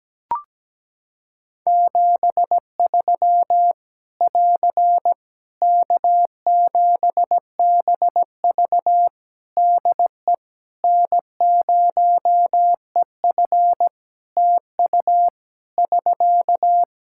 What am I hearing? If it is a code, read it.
Morse code, heard as a keyed tone, about 17 wpm, 700 Hz. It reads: 73 <AR> K7BV DE N0EF TU <SK>
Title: Practice QSO - Exchange 4 - Ending - 17wpm